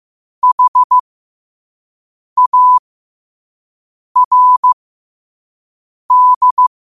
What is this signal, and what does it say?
Morse code, heard as a keyed tone, about 15 wpm, spaced as for 5 wpm, 1000 Hz. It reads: HARD
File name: qotdfiles / 5wpm